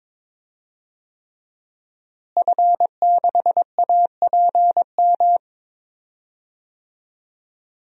Morse code, heard as a keyed tone, about 21 wpm, 700 Hz. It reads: F6APM
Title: DXCC-100 Call Signs - 5 Characters - Repeated 3x - 22wpm